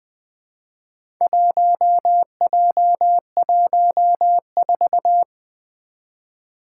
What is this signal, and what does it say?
Morse code, heard as a keyed tone, about 20 wpm, 700 Hz. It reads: 1J14
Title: Four Challenging Letter-Number Combinations 20wpm